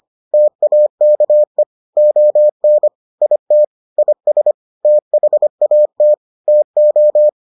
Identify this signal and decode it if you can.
Morse code, heard as a keyed tone, about 25 wpm, 600 Hz. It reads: TAKE ON IT IS THAT TO